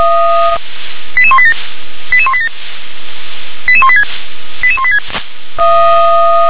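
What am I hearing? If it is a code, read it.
Five-tone selective call (ZVEI), heard as quick runs of five tones, tones of about 70 ms. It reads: 88187 88187 88186 88186
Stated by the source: ZVEI five-tone call received on SDR.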